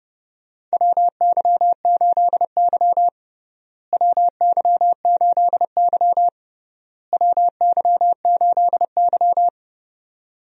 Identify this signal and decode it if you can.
Morse code, heard as a keyed tone, about 30 wpm, 700 Hz. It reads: WY8Y WY8Y WY8Y
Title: USA Call Signs - 4 Characters - Repeated 3x - 30wpm